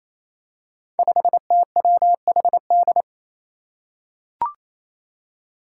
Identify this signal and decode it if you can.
Morse code, heard as a keyed tone, about 28 wpm, 700 Hz. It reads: 5TWHD
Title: Five Challenging Letter-Number Combinations 28wpm